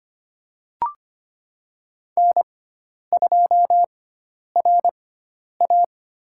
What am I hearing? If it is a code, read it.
Morse code, heard as a keyed tone, about 25 wpm, 700 Hz. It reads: N 2 R A
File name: Lesson 16g - Letter c - Call Signs 25wpm